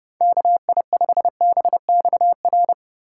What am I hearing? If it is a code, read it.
Morse code, heard as a keyed tone, about 34 wpm, 700 Hz. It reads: KI5BXR